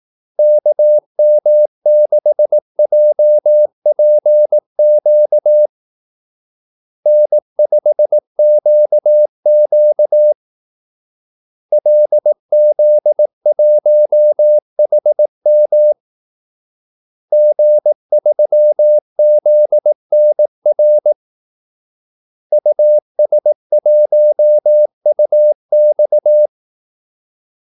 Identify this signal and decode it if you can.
Morse code, heard as a keyed tone, about 18 wpm, 600 Hz. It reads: KM6JPQ N5QQ LZ1HM G3ZNR US1UX
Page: Kaldesignaler 18 wpm